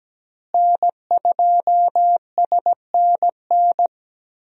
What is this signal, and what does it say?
Morse code, heard as a keyed tone, about 17 wpm, 700 Hz. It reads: N2SNN
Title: USA Call Signs - 5 Characters - Repeated 3x - 17wpm